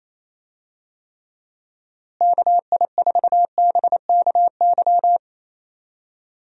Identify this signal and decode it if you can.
Morse code, heard as a keyed tone, about 28 wpm, 700 Hz. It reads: KI4BKY
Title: USA Call Signs - 6 Characters - Repeated 2x - 28wpm